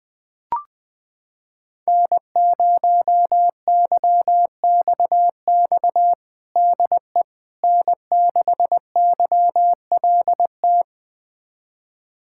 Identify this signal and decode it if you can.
Morse code, heard as a keyed tone, about 20 wpm, 700 Hz. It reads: N0YXX DE N6YLT
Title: USA Callsign DE Callsign - 5 Characters - 20wpm